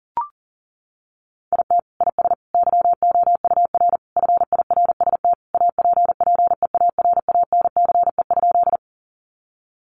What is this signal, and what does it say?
Morse code, heard as a keyed tone, about 40 wpm, 700 Hz. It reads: IT IS YOUR FIRST APPEARANCE?
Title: Sentences from Top 1100 Words 40wpm